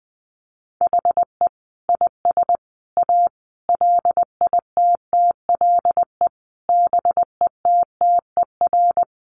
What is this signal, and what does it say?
Morse code, heard as a keyed tone, about 20 wpm, 700 Hz. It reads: HE IS A LITTLE BETTER
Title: Sentences from Top 600 Words 20wpm